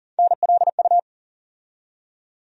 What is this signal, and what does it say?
Morse code, heard as a keyed tone, about 40 wpm, 700 Hz. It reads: NLU
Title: USA Call Sign Postfixes - 3 Letters 40wpm